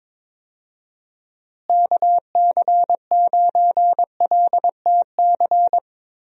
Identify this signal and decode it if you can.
Morse code, heard as a keyed tone, about 22 wpm, 700 Hz. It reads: KC9LTC
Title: USA Call Signs - 6 Characters - Repeated 2x - 22wpm